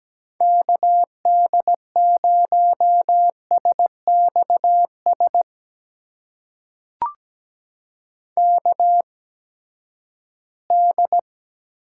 Morse code, heard as a keyed tone, about 17 wpm, spaced as for 12 wpm, 700 Hz. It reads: KD0SXS KD
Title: USA Call Signs - 6 Characters - 8x Character Spacing - 17wpm